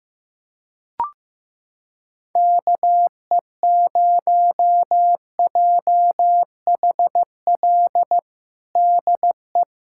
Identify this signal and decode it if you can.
Morse code, heard as a keyed tone, about 15 wpm, 700 Hz. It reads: KE0JHL DE
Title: USA Callsign DE Callsign 15wpm